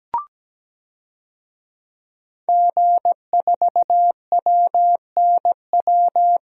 Morse code, heard as a keyed tone, about 17 wpm, 700 Hz. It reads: G4WNW
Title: DXCC-100 Call Signs - 5 Characters - Repeated 2x - 17wpm